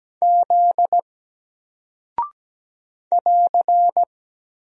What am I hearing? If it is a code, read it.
Morse code, heard as a keyed tone, about 17 wpm, 700 Hz. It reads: Z <AR>
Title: Single Character (Letter:Number:Punctuation:Prosign) 17wpm